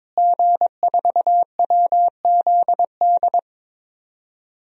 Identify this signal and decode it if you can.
Morse code, heard as a keyed tone, about 22 wpm, 700 Hz. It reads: G4WZD
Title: DXCC-100 Call Signs - 5 Characters - Repeated 2x - 22wpm